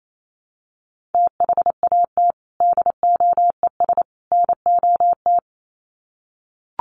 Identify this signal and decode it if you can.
Morse code, heard as a keyed tone, about 28 wpm, 700 Hz. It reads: THAT DOES NOT
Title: Sets of 3 Words 28wpm